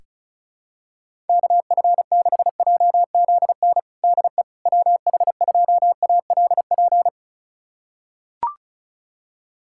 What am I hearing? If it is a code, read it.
Morse code, heard as a keyed tone, about 34 wpm, 700 Hz. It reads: KF6JZN DE WH2ALP
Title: USA Callsign DE Callsign - 6 Characters - 35wpm